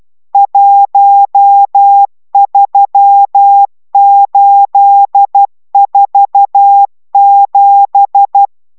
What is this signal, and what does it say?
Morse code, heard as a keyed tone, about 12 wpm, 800 Hz. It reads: 13847